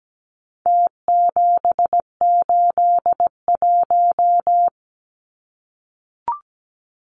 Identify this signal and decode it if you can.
Morse code, heard as a keyed tone, about 17 wpm, 700 Hz. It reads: T781
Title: Numbers - Contest Style - 4-Digit 17wpm